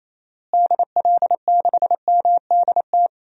Morse code, heard as a keyed tone, about 28 wpm, 700 Hz. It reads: DL6MDT
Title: DXCC-100 Call Signs - 6 Characters - Repeated 2x - 28wpm